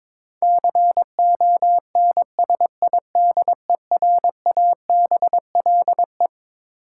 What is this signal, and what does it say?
Morse code, heard as a keyed tone, about 21 wpm, 700 Hz. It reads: CONSIDERABLE
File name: practice / 22wpm / Twelve Letter Words - 22wpm